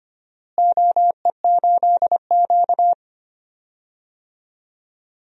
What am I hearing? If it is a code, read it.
Morse code, heard as a keyed tone, about 25 wpm, 700 Hz. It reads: OE8Q
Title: DXCC-100 Call Signs - 4 Characters - Repeated 3x - 25wpm